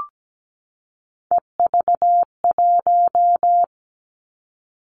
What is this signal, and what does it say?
Morse code, heard as a keyed tone, about 17 wpm, 700 Hz. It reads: EV1
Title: Three Challenging Letter-Number Combinations 17wpm